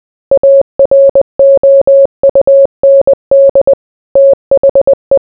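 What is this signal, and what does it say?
Morse code, heard as a keyed tone, about 20 wpm, 550 Hz. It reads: AROUND THE
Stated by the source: Morse code mixed with noise and music as background sound for sleep.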